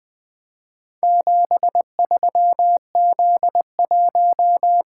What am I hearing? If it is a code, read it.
Morse code, heard as a keyed tone, about 20 wpm, 700 Hz. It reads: 73Z1
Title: Four Letter-Number Combinations - 20wpm